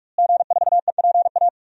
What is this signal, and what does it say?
Morse code, heard as a keyed tone, about 44 wpm, 700 Hz. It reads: G4EPA
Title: DXCC-100 Call Signs - 5 Characters - Repeated 2x - 45wpm